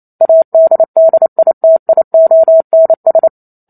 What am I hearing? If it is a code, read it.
Morse code, heard as a keyed tone, about 28 wpm, 650 Hz. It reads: ADDITIONS